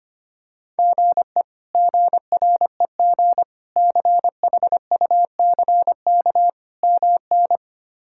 Morse code, heard as a keyed tone, about 25 wpm, 700 Hz. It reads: GE GREG CHUCK MN
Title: K1USN SST QSO Practice - 25wpm